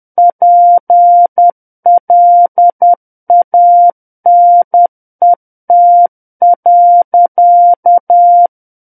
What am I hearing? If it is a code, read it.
Morse code, heard as a keyed tone, about 10 wpm, 700 Hz. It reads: PLANET.
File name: Book One / 01-Chapter 1 - 10 WPM